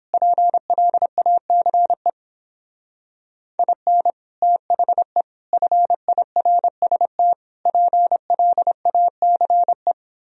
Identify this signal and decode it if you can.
Morse code, heard as a keyed tone, about 31 wpm, 700 Hz. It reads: PLACE IN THE FIRST PLACE